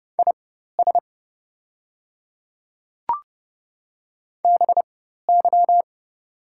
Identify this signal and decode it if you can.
Morse code, heard as a keyed tone, about 30 wpm, spaced as for 14 wpm, 700 Hz. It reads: IS BY
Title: 4x Character Spacing - Top 100 Words - 30wpm